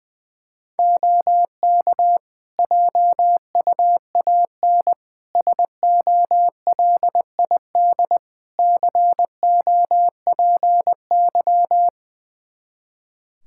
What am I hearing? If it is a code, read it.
Morse code, heard as a keyed tone, about 20 wpm, 700 Hz. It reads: OK JUAN SOLID COPY
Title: Practice QSO - Exchange 3 - More Info - 20wpm